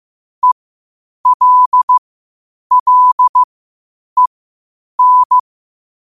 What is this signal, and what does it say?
Morse code, heard as a keyed tone, about 15 wpm, spaced as for 8 wpm, 1000 Hz. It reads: ELLEN
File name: qotdfiles / 8wpm